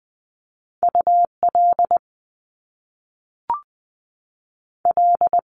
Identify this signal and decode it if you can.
Morse code, heard as a keyed tone, about 20 wpm, 700 Hz. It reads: U L L
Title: Top Two Letter Combinations in Words 20wpm